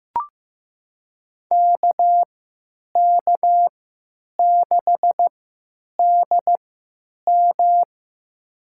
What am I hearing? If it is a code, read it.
Morse code, heard as a keyed tone, about 15 wpm, spaced as for 9 wpm, 700 Hz. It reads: KK6DM
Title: USA Call Signs - 5 Characters - 3x Character Spacing - 15wpm